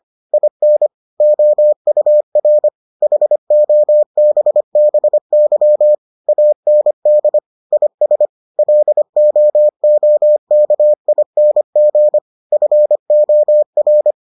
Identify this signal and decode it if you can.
Morse code, heard as a keyed tone, about 25 wpm, 600 Hz. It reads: IN OUR HOBBY AND IS LOOKING FOR